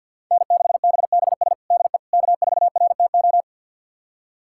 Morse code, heard as a keyed tone, about 49 wpm, 700 Hz. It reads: N6BBS DE K4RTO
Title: USA Callsign DE Callsign - 5 Characters - 50wpm